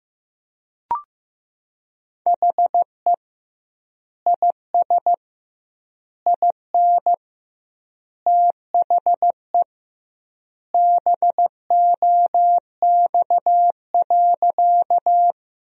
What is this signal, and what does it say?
Morse code, heard as a keyed tone, about 15 wpm, 700 Hz. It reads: HE IS IN THE BOX.
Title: 2x Word-Spacing - Sentences from Top 500 Words - 15wpm